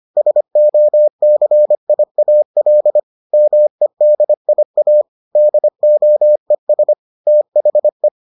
Morse code, heard as a keyed tone, about 25 wpm, 600 Hz. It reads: SOCIAL MEDIA DOES THE